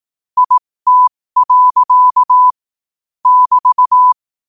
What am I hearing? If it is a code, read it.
Morse code, heard as a keyed tone, about 18 wpm, spaced as for 15 wpm, 1000 Hz. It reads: IT. <BT>
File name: qotdfiles / 15wpm